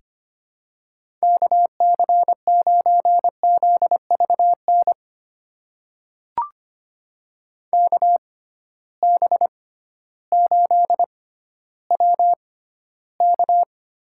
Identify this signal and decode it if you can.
Morse code, heard as a keyed tone, about 25 wpm, spaced as for 14 wpm, 700 Hz. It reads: KC9ZVN KB8WK
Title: USA Call Signs - 6 Characters - 6x Character Spacing - 25wpm